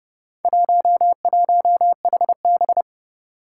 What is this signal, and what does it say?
Morse code, heard as a keyed tone, about 30 wpm, 700 Hz. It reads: 11HB
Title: Four Challenging Letter-Number Combinations 30wpm